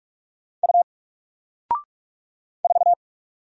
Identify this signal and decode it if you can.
Morse code, heard as a keyed tone, about 46 wpm, 700 Hz. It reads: U 4
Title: Single Letter-Number Alt 45wpm